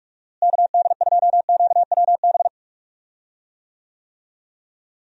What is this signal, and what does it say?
Morse code, heard as a keyed tone, about 47 wpm, 700 Hz. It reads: KD2QWB
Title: USA Call Signs - 6 Characters - Repeated 3x - 45wpm